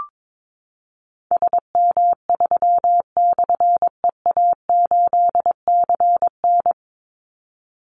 Morse code, heard as a keyed tone, about 22 wpm, 690 Hz. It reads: SM3/EA8CN
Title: Call Signs Repeated 2x - 22wpm